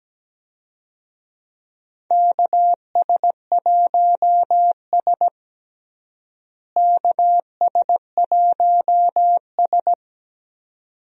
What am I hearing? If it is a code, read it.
Morse code, heard as a keyed tone, about 17 wpm, 700 Hz. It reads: KS1S KS1S
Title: USA Call Signs - 4 Characters - Repeated 2x - 17wpm